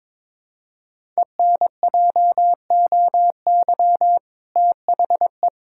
Morse code, heard as a keyed tone, about 21 wpm, 700 Hz. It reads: ENJOY THE
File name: practice / 22wpm / Sentences with 2-Word Phrases - 22wpm